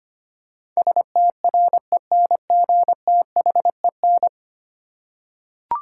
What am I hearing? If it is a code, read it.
Morse code, heard as a keyed tone, about 25 wpm, 700 Hz. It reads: STRENGTHEN
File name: 10 Letter Words - First 3 Letters Given - 25wpm